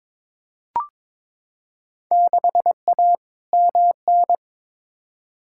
Morse code, heard as a keyed tone, about 21 wpm, 700 Hz. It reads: 6A MN
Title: Field Day Exchanges Encore - 22wpm